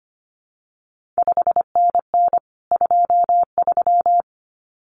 Morse code, heard as a keyed tone, about 25 wpm, 700 Hz. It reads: 5NN 23
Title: CQ WW DX Contest Exchange Encore - 25wpm